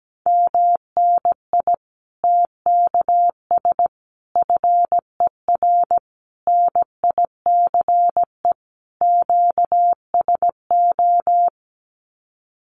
Morse code, heard as a keyed tone, about 17 wpm, 700 Hz. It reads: MNI TKS FER NICE QSO
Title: Common QSO Phrases 17wpm